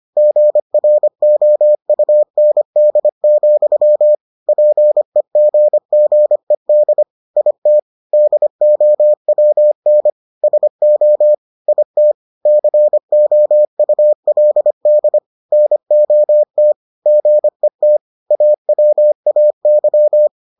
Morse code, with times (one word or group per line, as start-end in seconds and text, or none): 0.2-4.2 GROUND,
4.5-7.0 PEGGED
7.4-7.8 IT
8.1-10.1 DOWN
10.4-11.3 SO
11.7-12.1 IT
12.5-15.2 COULD
15.5-16.7 NOT
17.1-18.0 GET
18.3-20.3 AWAY